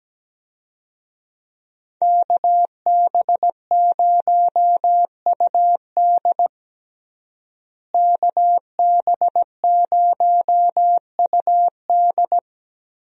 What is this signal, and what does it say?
Morse code, heard as a keyed tone, about 17 wpm, 700 Hz. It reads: KB0UD KB0UD
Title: USA Call Signs - 5 Characters - Repeated 3x - 17wpm